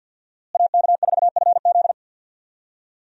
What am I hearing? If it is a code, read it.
Morse code, heard as a keyed tone, about 46 wpm, 700 Hz. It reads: AK4FZ
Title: USA Call Signs - 5 Characters - Repeated 2x - 50wpm